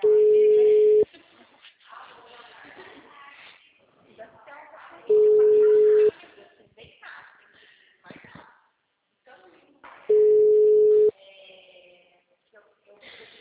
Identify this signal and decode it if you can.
Ringback tone (Europe, 425 Hz), with speech.
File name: out- -2011- - - .10528